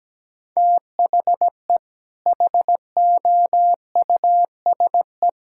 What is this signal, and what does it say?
Morse code, heard as a keyed tone, about 17 wpm, 700 Hz. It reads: THE HOUSE
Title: Sentences from Top 1600 Words 17wpm